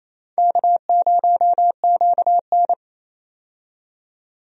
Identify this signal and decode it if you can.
Morse code, heard as a keyed tone, about 27 wpm, 700 Hz. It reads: K0QN
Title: USA Call Signs - 4 Characters - 28wpm